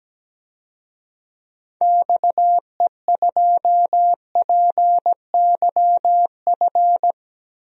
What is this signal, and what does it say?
Morse code, heard as a keyed tone, about 17 wpm, 700 Hz. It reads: XE2PYF
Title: DXCC-100 Call Signs - 6 Characters - Repeated 2x - 17wpm